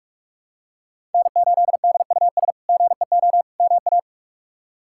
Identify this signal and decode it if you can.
Morse code, heard as a keyed tone, about 46 wpm, 700 Hz. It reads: N8DUS GEO MA
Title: K1USN SST QSO Practice - 45wpm